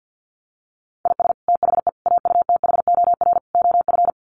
Morse code, heard as a keyed tone, about 50 wpm, 700 Hz. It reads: IS THE AUTHOR OF
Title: Sets of 4 Words 50wpm